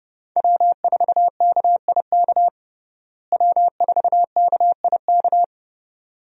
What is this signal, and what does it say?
Morse code, heard as a keyed tone, about 30 wpm, 700 Hz. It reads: W4KIK W4KIK
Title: USA Call Signs - 5 Characters - Repeated 2x - 30wpm